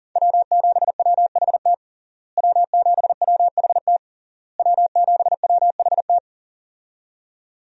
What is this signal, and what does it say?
Morse code, heard as a keyed tone, about 37 wpm, 700 Hz. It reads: W7WHT W7WHT W7WHT
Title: USA Call Signs - 5 Characters - Repeated 3x - 40wpm